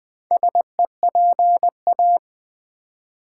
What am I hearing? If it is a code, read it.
Morse code, heard as a keyed tone, about 20 wpm, 700 Hz. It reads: SEPA
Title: Build A Word - 2 Letters At A Time - 8-Letter Words - 20wpm